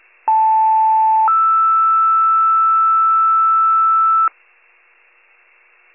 Two-tone page, heard as a one-second tone, then a three-second tone, about 880 then 1320 Hz, with noise. It is heard from 0.3 to 4.3 s.
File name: wcalert